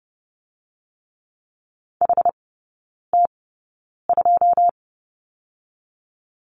Morse code, heard as a keyed tone, about 30 wpm, spaced as for 9 wpm, 700 Hz. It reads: HT2
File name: Lesson 17e - Letter u - Three Characters 30wpm